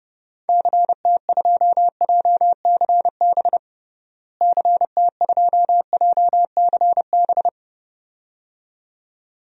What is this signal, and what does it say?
Morse code, heard as a keyed tone, about 32 wpm, 700 Hz. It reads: CT2JCB CT2JCB
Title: DXCC-100 Call Signs - 6 Characters - Repeated 2x - 30wpm